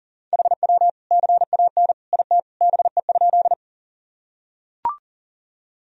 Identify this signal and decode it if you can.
Morse code, heard as a keyed tone, about 40 wpm, 700 Hz. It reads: HW CAN IT BE?
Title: Sentences from Top 100 Words and Abbreviations 40wpm